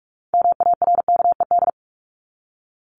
Morse code, heard as a keyed tone, about 44 wpm, 700 Hz. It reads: MARKED
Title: ED Suffix - Top 100 Words with Base Words 45wpm